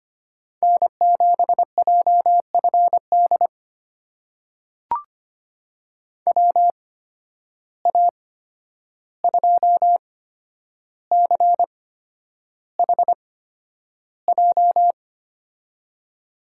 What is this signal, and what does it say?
Morse code, heard as a keyed tone, about 25 wpm, spaced as for 10 wpm, 700 Hz. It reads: N7JFD WA2CHJ
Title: USA Call Signs - 8x Character Spacing - 25wpm